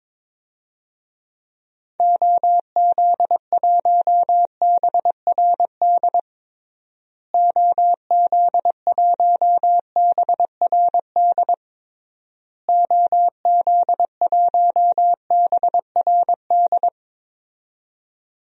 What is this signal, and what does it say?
Morse code, heard as a keyed tone, about 21 wpm, 700 Hz. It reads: OZ1BRD OZ1BRD OZ1BRD
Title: DXCC-100 Call Signs - 6 Characters - Repeated 3x - 22wpm